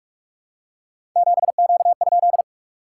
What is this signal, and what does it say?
Morse code, heard as a keyed tone, about 42 wpm, 690 Hz. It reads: 7Q?
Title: SOTA QSO Practice 45wpm